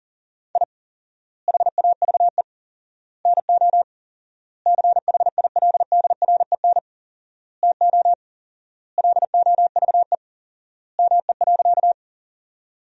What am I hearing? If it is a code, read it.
Morse code, heard as a keyed tone, about 40 wpm, 700 Hz. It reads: I HAVE NO CHILDREN TO LOVE ME.
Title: 4x Word-Spacing - Sentences from Top 400 Words - 40wpm